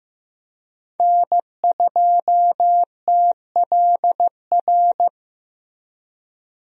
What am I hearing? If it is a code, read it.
Morse code, heard as a keyed tone, about 16 wpm, 700 Hz. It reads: N2TLR
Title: USA Call Signs - 5 Characters - Repeated 2x - 15wpm